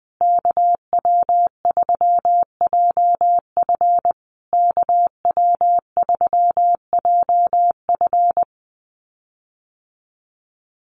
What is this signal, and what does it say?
Morse code, heard as a keyed tone, about 20 wpm, 700 Hz. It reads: KW3JF KW3JF
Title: USA Call Signs Repeated 3x 20wpm